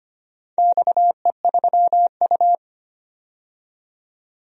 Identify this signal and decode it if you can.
Morse code, heard as a keyed tone, about 25 wpm, 700 Hz. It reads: XE3U
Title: DXCC-100 Call Signs - 4 Characters - 25wpm